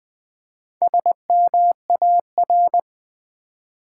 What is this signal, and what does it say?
Morse code, heard as a keyed tone, about 20 wpm, 700 Hz. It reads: SMAR
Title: Build A Word - 4 Letters At A Time - 10-Letter Words - 20wpm